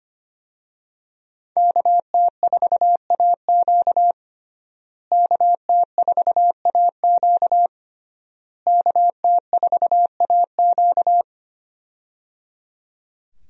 Morse code, heard as a keyed tone, about 25 wpm, 700 Hz. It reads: KT4AQ KT4AQ KT4AQ
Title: USA Call Signs - 5 Characters - Repeated 3x - 25wpm